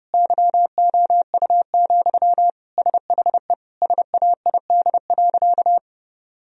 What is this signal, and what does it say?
Morse code, heard as a keyed tone, about 32 wpm, 700 Hz. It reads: YOU, SHE SAID.
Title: Sentences from Top 1000 Words - Review Part 1 - 30wpm